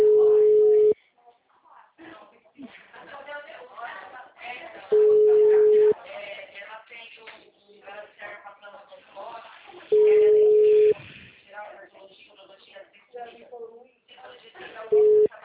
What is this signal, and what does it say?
Ringback tone (Europe, 425 Hz), with speech.
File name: out- -2026